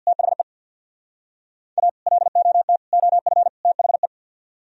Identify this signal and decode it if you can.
Morse code, heard as a keyed tone, about 61 wpm, 700 Hz. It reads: THE A LOT OF THE
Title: Speed-Racing - Sets of 4 Words 50wpm